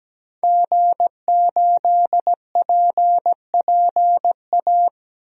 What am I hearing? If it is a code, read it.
Morse code, heard as a keyed tone, about 17 wpm, 700 Hz. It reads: G8PPA
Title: DXCC-100 Call Signs - 5 Characters - 17wpm